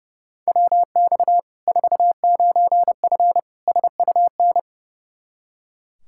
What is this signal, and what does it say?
Morse code, heard as a keyed tone, about 30 wpm, 700 Hz. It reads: WX 49F SUN
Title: Practice QSO - Exchange 2 - Info - 30wpm